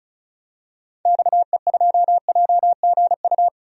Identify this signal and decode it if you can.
Morse code, heard as a keyed tone, about 34 wpm, 700 Hz. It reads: XE2JGU
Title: DXCC-100 Call Signs - 6 Characters - Repeated 2x - 35wpm